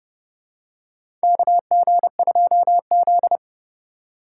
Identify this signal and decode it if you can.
Morse code, heard as a keyed tone, about 32 wpm, 700 Hz. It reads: KG2Z
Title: USA Call Signs - 2x Character Spacing - 30wpm